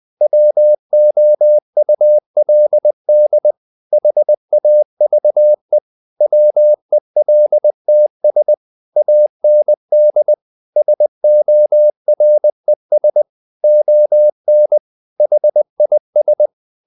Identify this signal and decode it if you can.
Morse code, heard as a keyed tone, about 20 wpm, 600 Hz. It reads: WOULD HAVE WELTS AND SORES ON HIS
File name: the old man and the sea / the old man and the sea part 1 20wpm